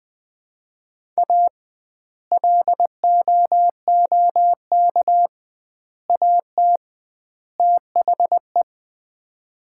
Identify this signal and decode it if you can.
Morse code, heard as a keyed tone, about 20 wpm, 700 Hz. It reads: A LOOK AT THE
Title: 2x Word-Spacing - Sets of 4 Words - 20wpm